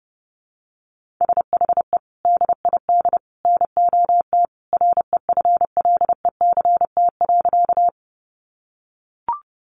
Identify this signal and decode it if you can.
Morse code, heard as a keyed tone, about 30 wpm, 700 Hz. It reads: SHE DID NOT REFLECT.
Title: Sentences from Top 1500 Words 30wpm